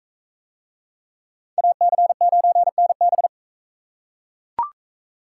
Morse code, heard as a keyed tone, about 40 wpm, 700 Hz. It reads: AC9NB